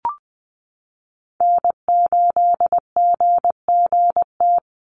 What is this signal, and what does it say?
Morse code, heard as a keyed tone, about 20 wpm, 700 Hz. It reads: N8GGT
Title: USA Call Signs 20wpm